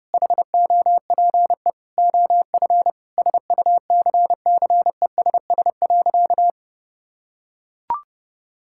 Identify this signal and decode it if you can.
Morse code, heard as a keyed tone, about 30 wpm, 700 Hz. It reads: HOPE OF SUCCESS.
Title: Sentences from Top 1000 Words - Review Part 6 - 30wpm